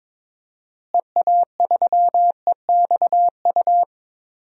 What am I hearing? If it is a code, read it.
Morse code, heard as a keyed tone, about 22 wpm, 700 Hz. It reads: EA3EXU
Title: DXCC-100 Call Signs - 6 Characters - Repeated 2x - 22wpm